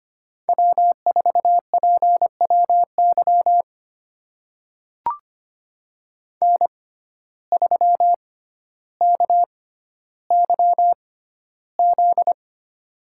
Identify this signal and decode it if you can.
Morse code, heard as a keyed tone, about 25 wpm, spaced as for 13 wpm, 700 Hz. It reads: W4PWY N3KYZ
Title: USA Call Signs - 5 Characters - 6x Character Spacing - 25wpm